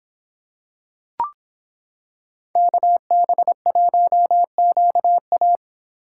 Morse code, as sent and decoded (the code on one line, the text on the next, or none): -.- -... .---- --.- .-
KB1QA